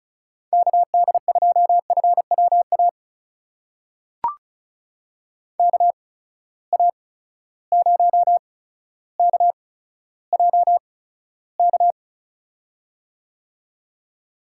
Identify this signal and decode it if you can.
Morse code, heard as a keyed tone, about 36 wpm, spaced as for 15 wpm, 700 Hz. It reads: KD2FWA KA0KJK
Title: USA Call Signs - 6 Characters - 8x Character Spacing - 35wpm